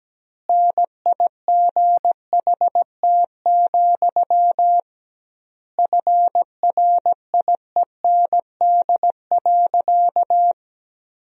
Morse code, heard as a keyed tone, about 17 wpm, 700 Hz. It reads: NIGHT, FRIEND.
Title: 2x Word-Spacing - Sentences from Top 300 Words - 17wpm